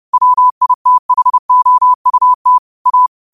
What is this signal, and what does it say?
Morse code, heard as a keyed tone, about 30 wpm, 1000 Hz. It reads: WITHOUT A
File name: qotdfiles / 30wpm